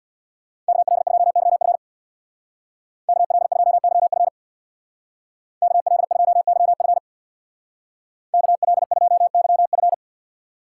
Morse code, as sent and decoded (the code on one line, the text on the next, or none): -.- .-.. ..--- -.-- ..-. / -.- .-.. ..--- -.-- ..-. / -.- .-.. ..--- -.-- ..-. / -.- .-.. ..--- -.-- ..-.
KL2YF KL2YF KL2YF KL2YF